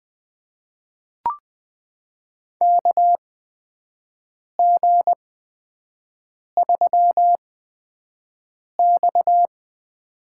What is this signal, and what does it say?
Morse code, heard as a keyed tone, about 20 wpm, spaced as for 5 wpm, 700 Hz. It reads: KG3X
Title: USA Call Signs - 8x Character Spacing - 20wpm